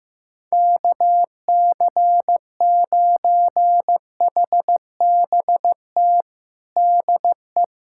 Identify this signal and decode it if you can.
Morse code, heard as a keyed tone, about 15 wpm, 700 Hz. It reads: KC9HBT DE
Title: USA Callsign DE Callsign 15wpm